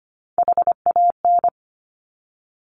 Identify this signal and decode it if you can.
Morse code, heard as a keyed tone, about 25 wpm, 700 Hz. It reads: HAN
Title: Top Three Letter Combinations in Words 25wpm